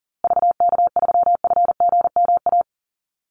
Morse code, heard as a keyed tone, about 39 wpm, 700 Hz. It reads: VK3FGMA
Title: USA Call Signs Repeated 2x 40wpm